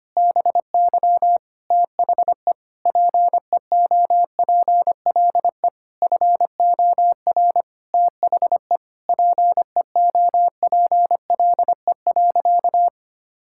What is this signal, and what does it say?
Morse code, heard as a keyed tone, about 25 wpm, 700 Hz. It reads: BY THE PEOPLE FOR THE PEOPLE.
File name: practice / 25wpm / Sentences with 9-Word Phrases - 25wpm